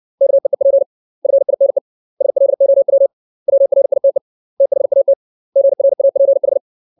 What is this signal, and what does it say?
Morse code, heard as a keyed tone, about 60 wpm, 550 Hz. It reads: KEEP FINE SLOW WRITE THAT GRAPH
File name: 632 words / 632 Words 60WPM